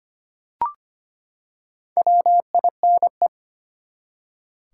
Morse code, heard as a keyed tone, about 25 wpm, 700 Hz. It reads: WINE
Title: Four Letter Words - 25wpm